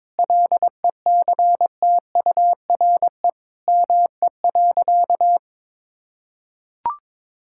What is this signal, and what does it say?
Morse code, heard as a keyed tone, about 22 wpm, 700 Hz. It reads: LECTURE ME.
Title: Sentences from Top 1600 Words 22wpm